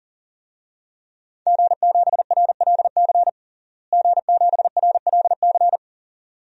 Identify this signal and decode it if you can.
Morse code, heard as a keyed tone, about 37 wpm, 700 Hz. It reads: G7RLC G7RLC
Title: DXCC-100 Call Signs - 5 Characters - Repeated 2x - 40wpm